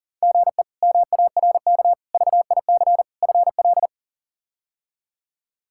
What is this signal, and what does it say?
Morse code, heard as a keyed tone, about 37 wpm, 700 Hz. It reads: GE MARK VIC FL
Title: K1USN SST QSO Practice - 40wpm